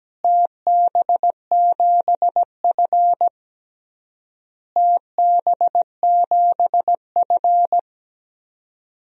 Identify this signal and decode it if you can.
Morse code, heard as a keyed tone, about 17 wpm, 700 Hz. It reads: TB7F TB7F
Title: DXCC-100 Call Signs - 4 Characters - Repeated 3x - 17wpm